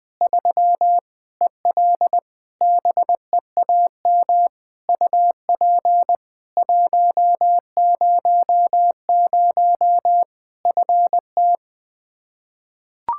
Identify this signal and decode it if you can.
Morse code, heard as a keyed tone, about 20 wpm, 700 Hz. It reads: 3 EL BEAM UP 100 FT
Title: Practice QSO - Exchange 2 - Info - 20wpm